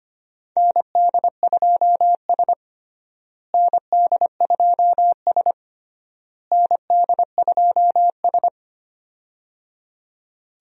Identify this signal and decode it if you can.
Morse code, heard as a keyed tone, about 25 wpm, 700 Hz. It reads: ND2S ND2S ND2S
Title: USA Call Signs - 4 Characters - Repeated 3x - 25wpm